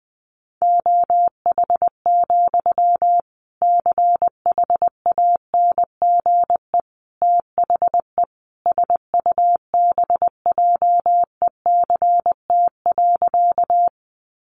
Morse code, heard as a keyed tone, about 20 wpm, 700 Hz. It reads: OH, CHANGE THE SUBJECT.
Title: Sentences from Top 800 Words 20wpm